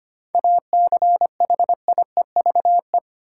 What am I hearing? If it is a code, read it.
Morse code, heard as a keyed tone, about 25 wpm, 700 Hz. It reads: ACHIEVE
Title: Sentences from Top 1000 Words - Review Part 5 - New Word Drill - 25wpm